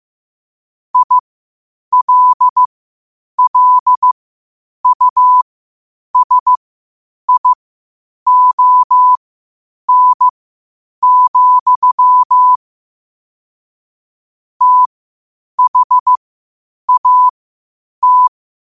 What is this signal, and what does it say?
Morse code, heard as a keyed tone, about 15 wpm, spaced as for 8 wpm, 1000 Hz. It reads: ILLUSION, THAT
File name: qotdfiles / 8wpm